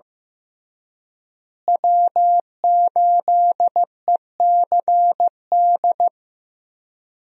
Morse code, heard as a keyed tone, about 15 wpm, 700 Hz. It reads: W8ECD
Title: USA Call Signs - 5 Characters - Repeated 3x - 15wpm